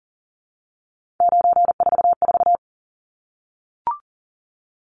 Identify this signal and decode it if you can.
Morse code, heard as a keyed tone, about 40 wpm, 700 Hz. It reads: 944
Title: Numbers - 3-Digit 40wpm